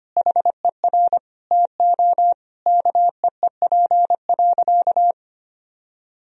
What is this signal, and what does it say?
Morse code, heard as a keyed tone, about 25 wpm, 700 Hz. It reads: HER TO KEEP.
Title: Sentences from Top 1000 Words - Review Part 2 - 25wpm